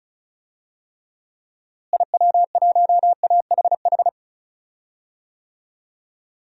Morse code, heard as a keyed tone, about 34 wpm, 700 Hz. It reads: IW1AHH
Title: DXCC-100 Call Signs - 6 Characters - Repeated 2x - 35wpm